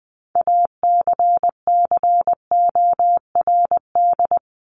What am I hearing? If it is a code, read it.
Morse code, heard as a keyed tone, about 20 wpm, 700 Hz. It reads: ACCORD
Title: ING Suffix - Top 100 Words with Base Words 20wpm